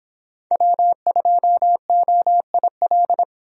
Morse code, heard as a keyed tone, about 24 wpm, 700 Hz. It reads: W2OIL